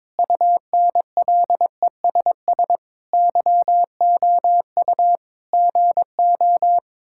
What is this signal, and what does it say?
Morse code, heard as a keyed tone, about 22 wpm, 700 Hz. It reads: UNLESS YOU GO
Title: Sentences from Top 1000 Words - Review Part 6 - 22wpm